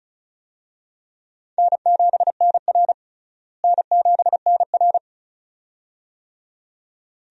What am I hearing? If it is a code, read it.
Morse code, heard as a keyed tone, about 34 wpm, 700 Hz. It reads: N7NR N7NR
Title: USA Call Signs - 4 Characters - Repeated 2x - 35wpm